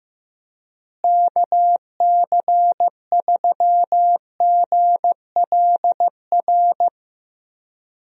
Morse code, heard as a keyed tone, about 15 wpm, 700 Hz. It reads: KC3GLR
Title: USA Call Signs - 6 Characters - Repeated 3x - 15wpm